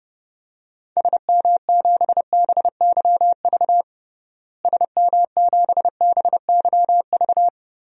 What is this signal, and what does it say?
Morse code, heard as a keyed tone, about 32 wpm, 700 Hz. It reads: SM7BYV SM7BYV
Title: DXCC-100 Call Signs - 6 Characters - Repeated 3x - 30wpm